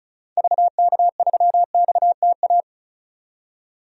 Morse code, heard as a keyed tone, about 34 wpm, 700 Hz. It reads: VK3XTA
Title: DXCC-100 Call Signs - 6 Characters - 35wpm